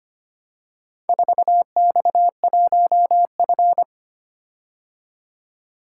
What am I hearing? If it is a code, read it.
Morse code, heard as a keyed tone, about 25 wpm, 700 Hz. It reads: 4X1F
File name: practice / 25wpm / Four Letter-Number Combinations - 25wpm